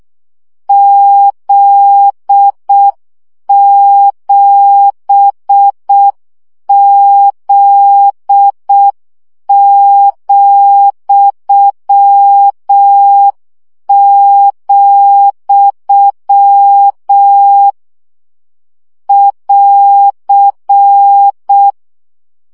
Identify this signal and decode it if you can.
Morse code, heard as a keyed tone, about 6 wpm, 800 Hz. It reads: Z7Z,, <AR>
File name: CW gyakorlas z 7 vesszo 5 baud